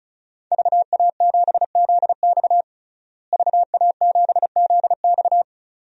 Morse code, heard as a keyed tone, about 36 wpm, 700 Hz. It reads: VA7ZX VA7ZX
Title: DXCC-100 Call Signs - 5 Characters - Repeated 2x - 35wpm